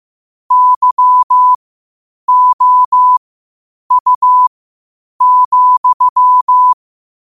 Morse code, heard as a keyed tone, about 15 wpm, spaced as for 8 wpm, 1000 Hz. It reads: YOU,
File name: qotdfiles / 8wpm